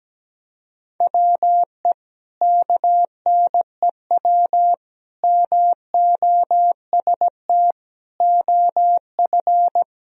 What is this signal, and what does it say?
Morse code, heard as a keyed tone, about 17 wpm, 700 Hz. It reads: WE KNEW MOST OF
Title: Sentences from Top 1000 Words - Review Part 1 - 17wpm